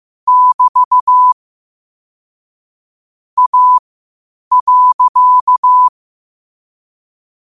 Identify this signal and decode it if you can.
Morse code, heard as a keyed tone, about 15 wpm, spaced as for 8 wpm, 1000 Hz. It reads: <BT> A.
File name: qotdfiles / 8wpm